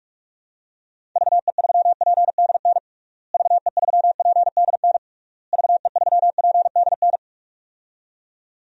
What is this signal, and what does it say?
Morse code, heard as a keyed tone, about 46 wpm, 700 Hz. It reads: VE3PDN VE3PDN VE3PDN
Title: DXCC-100 Call Signs - 6 Characters - Repeated 3x - 45wpm